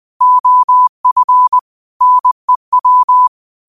Morse code, heard as a keyed tone, about 20 wpm, 1000 Hz. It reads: OF NEW